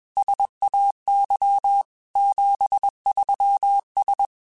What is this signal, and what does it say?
Morse code, heard as a keyed tone, about 21 wpm, 800 Hz. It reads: SAY 73S